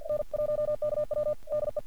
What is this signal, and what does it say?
Morse code, heard as a keyed tone, about 46 wpm, 600 Hz. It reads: N1KW DE
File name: archived / CRSnet